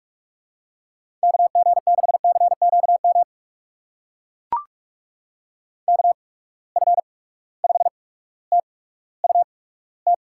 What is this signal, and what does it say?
Morse code, heard as a keyed tone, about 44 wpm, spaced as for 19 wpm, 700 Hz. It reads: KG6CQM KF5TUT
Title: USA Call Signs - 8x Character Spacing - 45wpm